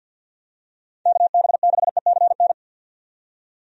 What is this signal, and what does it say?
Morse code, heard as a keyed tone, about 52 wpm, 700 Hz. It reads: KB6ECN
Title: USA Call Signs - 6 Characters - 50wpm